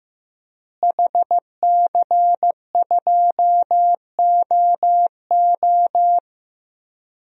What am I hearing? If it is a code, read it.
Morse code, heard as a keyed tone, about 15 wpm, 700 Hz. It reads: HC2OO
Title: DXCC-100 Call Signs - 5 Characters - Repeated 2x - 15wpm